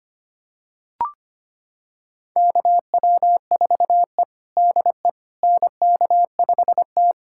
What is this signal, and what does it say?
Morse code, heard as a keyed tone, about 25 wpm, 700 Hz. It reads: KW4E DE NK5T
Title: USA Callsign DE Callsign - 4 Characters - 25wpm